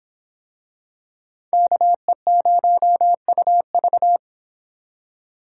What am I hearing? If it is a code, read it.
Morse code, heard as a keyed tone, about 26 wpm, 700 Hz. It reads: KE0UV